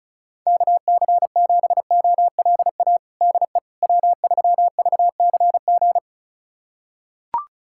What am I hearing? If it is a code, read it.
Morse code, heard as a keyed tone, about 34 wpm, 700 Hz. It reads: KC7OLA DE W3VCG
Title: USA Callsign DE Callsign 35wpm